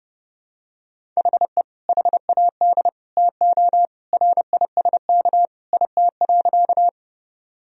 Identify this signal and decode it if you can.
Morse code, heard as a keyed tone, about 30 wpm, 700 Hz. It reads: HE HAD TO RISK IT.
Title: Sentences from Top 1000 Words - Review Part 4 - 30wpm